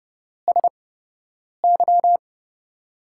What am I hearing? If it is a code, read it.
Morse code, heard as a keyed tone, about 30 wpm, spaced as for 8 wpm, 700 Hz. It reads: SY
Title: USA Call Signs - 6 Characters - 8x Character Spacing - 30wpm